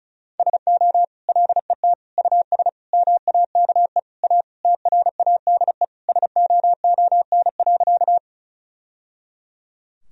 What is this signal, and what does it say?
Morse code, heard as a keyed tone, about 36 wpm, 700 Hz. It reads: SO LET US MAKE A TRADE SOON.
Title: Sentences from Top 1000 Words - Review Part 4 - 35wpm